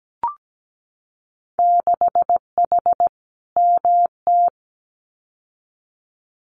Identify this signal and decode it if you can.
Morse code, heard as a keyed tone, about 17 wpm, 700 Hz. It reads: 6H MT
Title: Winter Field Day Exchanges Encore 17wpm